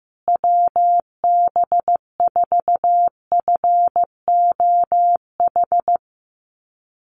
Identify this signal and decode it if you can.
Morse code, heard as a keyed tone, about 15 wpm, 700 Hz. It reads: WB4FOH
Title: USA Call Signs Repeated 3x 15wpm